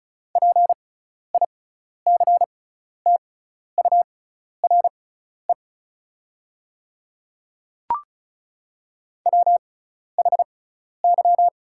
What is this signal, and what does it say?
Morse code, heard as a keyed tone, about 36 wpm, spaced as for 12 wpm, 690 Hz. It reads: PICTURE WHY
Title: 6x Character Spacing - Top 300 Words - 35wpm